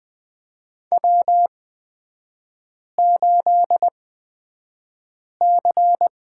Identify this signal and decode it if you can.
Morse code, heard as a keyed tone, about 20 wpm, spaced as for 5 wpm, 700 Hz. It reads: W8C
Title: 224 Lesson 36e - BT - Three Characters 20wpm